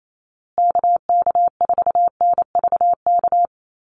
Morse code, heard as a keyed tone, about 28 wpm, 700 Hz. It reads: KK4NVK
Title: USA Call Signs Repeated 2x 28wpm